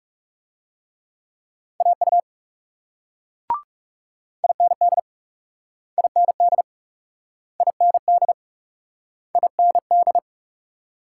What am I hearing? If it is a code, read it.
Morse code, heard as a keyed tone, about 38 wpm, 700 Hz. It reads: AU IND IND IND IND